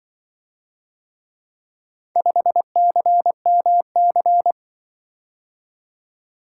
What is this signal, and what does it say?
Morse code, heard as a keyed tone, about 24 wpm, 700 Hz. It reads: 5CMC
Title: Speed-Racing - Four Letter-Number Combinations - 20wpm